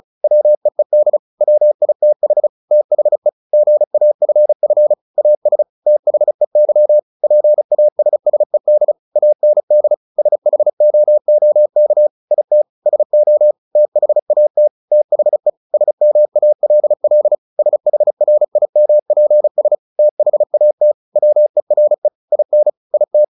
Morse code, heard as a keyed tone, about 34 wpm, 600 Hz. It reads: WEED WITH THE GAFF AS THEY PASSED AND SHOOK IT SO THAT THE SMALL SHRIMPS THAT WERE IN IT
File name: the old man and the sea 2 35wpm